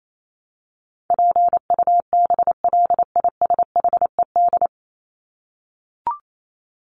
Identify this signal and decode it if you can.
Morse code, heard as a keyed tone, about 28 wpm, 700 Hz. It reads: PUBLISHED
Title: ED Suffix - Top 100 Words with Base Words 28wpm